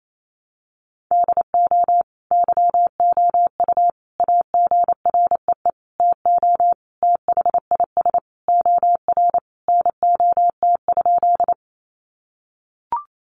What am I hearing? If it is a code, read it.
Morse code, heard as a keyed tone, about 28 wpm, 700 Hz. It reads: DO YOU AGREE TO THIS OR NOT?
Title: Sentences from Top 1100 Words 28wpm